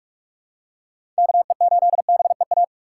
Morse code, heard as a keyed tone, about 46 wpm, 700 Hz. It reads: KE8BEA
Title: USA Call Signs - 6 Characters - 3x Character Spacing - 45wpm